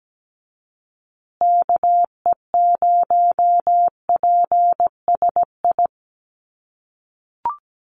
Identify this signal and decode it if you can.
Morse code, heard as a keyed tone, about 17 wpm, 700 Hz. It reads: KE0PSI
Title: USA Call Signs Repeated 2x 17wpm